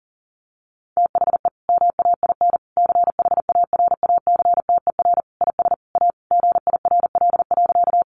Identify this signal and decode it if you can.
Morse code, heard as a keyed tone, about 40 wpm, 700 Hz. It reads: THE MAIN CHARACTER IS A GIRL.
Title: Additional Easy Phrases 40wpm